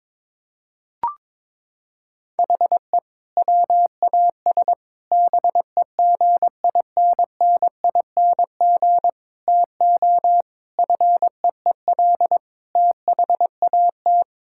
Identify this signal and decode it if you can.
Morse code, heard as a keyed tone, about 22 wpm, 700 Hz. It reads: HE WAS BEGINNING TO FEEL THAT
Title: Sentences from Top 1000 Words - Review Part 5 - 22wpm